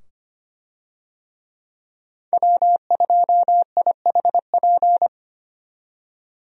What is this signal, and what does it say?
Morse code, heard as a keyed tone, about 25 wpm, 700 Hz. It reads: W2IHP
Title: USA Call Signs - 5 Characters - 25wpm